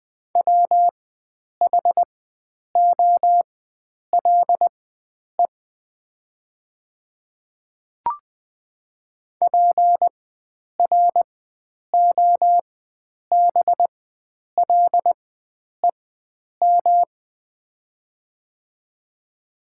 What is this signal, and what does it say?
Morse code, heard as a keyed tone, about 20 wpm, spaced as for 9 wpm, 700 Hz. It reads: WHOLE PROBLEM
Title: 4x Character Spacing - Top 500 Words - 20wpm